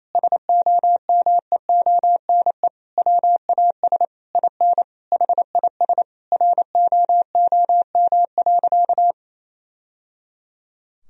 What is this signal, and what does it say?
Morse code, heard as a keyed tone, about 28 wpm, 700 Hz. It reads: SOMEONE WAS IN HIS ROOM.
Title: Sentences from Top 1000 Words - Review Part 4 - 28wpm